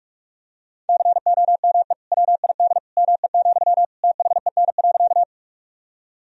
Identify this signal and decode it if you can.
Morse code, heard as a keyed tone, about 46 wpm, 700 Hz. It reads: COME WID ME, THEN.
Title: Sentences from Top 200 Words and Abbreviations 45wpm